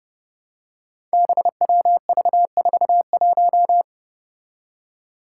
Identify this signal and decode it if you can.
Morse code, heard as a keyed tone, about 31 wpm, 700 Hz. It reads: BWV41
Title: Five Challenging Letter-Number Combinations 30wpm